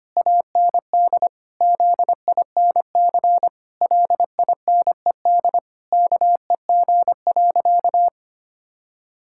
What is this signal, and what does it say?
Morse code, heard as a keyed tone, about 25 wpm, 700 Hz. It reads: AND ZINC LINED KEG.
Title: Pangram Challenge v2 - 25wpm